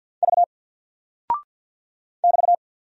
Morse code, heard as a keyed tone, about 48 wpm, 700 Hz. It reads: V <BT>
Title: Single Character (Letter:Number:Punctuation:Prosign) Alt 50wpm